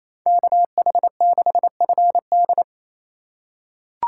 Morse code, heard as a keyed tone, about 28 wpm, 700 Hz. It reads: KH6FD
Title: USA Call Signs - 5 Characters - 2x Character Spacing - 28wpm